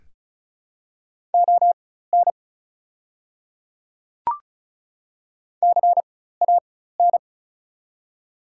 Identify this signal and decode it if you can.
Morse code, heard as a keyed tone, about 34 wpm, spaced as for 16 wpm, 700 Hz. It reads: ON CAN
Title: 4x Character Spacing - Top 100 Words - 35wpm